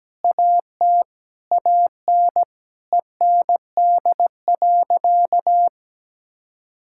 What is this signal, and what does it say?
Morse code, heard as a keyed tone, about 17 wpm, 700 Hz. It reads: AT AN END.
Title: Sentences from Top 1600 Words 17wpm